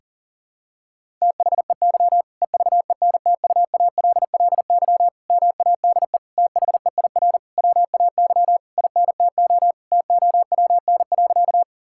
Morse code, heard as a keyed tone, about 40 wpm, 700 Hz. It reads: THEY EVENTUALLY MADE THEIR WAY INTO TOWN.
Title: Sentences from Top 1000 Words - Review Part 6 - 40wpm